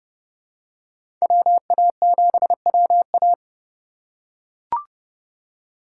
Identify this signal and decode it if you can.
Morse code, heard as a keyed tone, about 30 wpm, 700 Hz. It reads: WA7WA
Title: USA Call Signs - 5 Characters - 4x Character Spacing - 30wpm